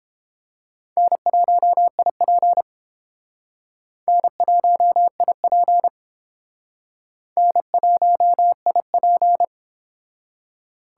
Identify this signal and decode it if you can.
Morse code, heard as a keyed tone, about 31 wpm, 700 Hz. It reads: N1IP N1IP N1IP